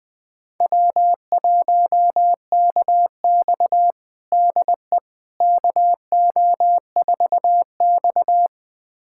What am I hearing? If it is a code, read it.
Morse code, heard as a keyed tone, about 21 wpm, 700 Hz. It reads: W1KX DE KO4X
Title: USA Callsign DE Callsign - 4 Characters - 20wpm